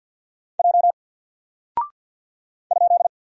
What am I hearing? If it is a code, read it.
Morse code, heard as a keyed tone, about 50 wpm, 700 Hz. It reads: J ?
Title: Single Character (Letter:Number:Punctuation:Prosign) 50wpm